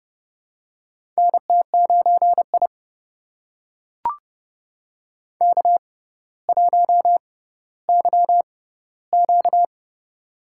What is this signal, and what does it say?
Morse code, heard as a keyed tone, about 30 wpm, spaced as for 15 wpm, 700 Hz. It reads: NT9I K1YQ
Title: USA Call Signs - 4 Characters - 6x Character Spacing - 30wpm